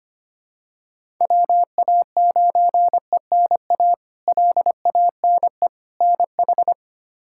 Morse code, heard as a keyed tone, about 25 wpm, 700 Hz. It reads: WA9ENA LANE NH
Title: K1USN SST QSO Practice - 25wpm